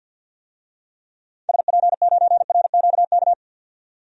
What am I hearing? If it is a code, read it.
Morse code, heard as a keyed tone, about 52 wpm, 700 Hz. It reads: SP9RQK
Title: DXCC-100 Call Signs - 6 Characters - 50wpm